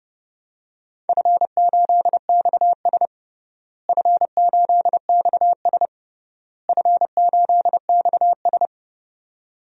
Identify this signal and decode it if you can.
Morse code, heard as a keyed tone, about 28 wpm, 700 Hz. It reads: F8XS F8XS F8XS
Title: DXCC-100 Call Signs - 4 Characters - Repeated 3x - 30wpm